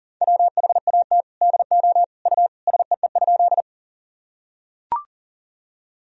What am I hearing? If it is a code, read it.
Morse code, heard as a keyed tone, about 40 wpm, 700 Hz. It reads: WHAT DO U SEE?
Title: Sentences from Top 100 Words and Abbreviations 40wpm